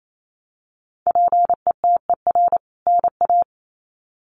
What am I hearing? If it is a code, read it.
Morse code, heard as a keyed tone, about 28 wpm, 700 Hz. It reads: PETER NA
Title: CWT Exchange 28wpm